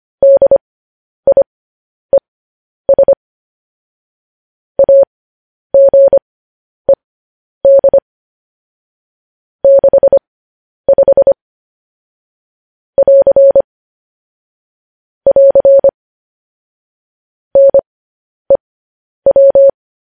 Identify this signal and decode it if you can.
Morse code, heard as a keyed tone, about 25 wpm, spaced as for 10 wpm, 550 Hz. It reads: DIES AGED 65 <AR> <AR> NEW
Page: news headlines in morse code at 10 wpm